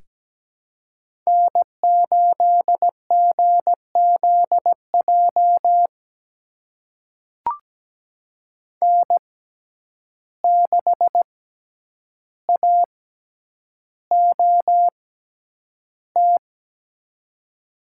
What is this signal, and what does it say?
Morse code, heard as a keyed tone, about 17 wpm, spaced as for 9 wpm, 700 Hz. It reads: N8GZJ N6AOT
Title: USA Call Signs - 5 Characters - 6x Character Spacing - 17wpm